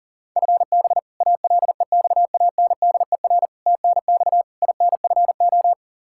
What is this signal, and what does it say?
Morse code, heard as a keyed tone, about 40 wpm, 700 Hz. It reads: FB ALEXANDER TNX INFO
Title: K1USN SST QSO Practice 40wpm